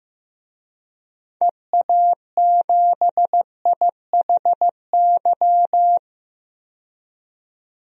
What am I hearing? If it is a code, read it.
Morse code, heard as a keyed tone, about 15 wpm, 700 Hz. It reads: EA7IHY
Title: DXCC-100 Call Signs - 6 Characters - 15wpm